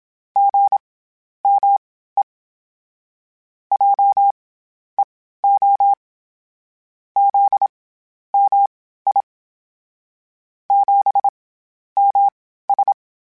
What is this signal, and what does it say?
Morse code, heard as a keyed tone, about 27 wpm, spaced as for 12 wpm, 800 Hz. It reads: GME JEO ZMI 7MS
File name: CW Comparisons / CW Comparisons-mp3s / CW Comparisons Broken Characters @25